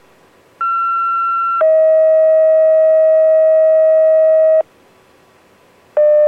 Two-tone page, heard as a one-second tone, then a three-second tone, about 1360 then 640 Hz. It is heard from 0.6 to 4.6 s.